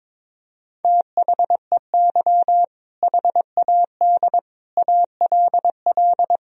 Morse code, heard as a keyed tone, about 22 wpm, 700 Hz. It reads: THEY HAD ALL
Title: Sentences from Top 200 Words and Abbreviations 22wpm